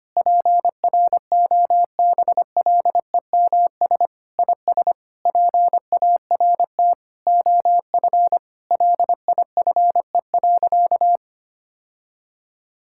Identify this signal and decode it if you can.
Morse code, heard as a keyed tone, about 25 wpm, 700 Hz. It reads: PROBLEMS IS PART OF LIFE.
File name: practice / 25wpm / Sentences with 3-Word Phrases - 25wpm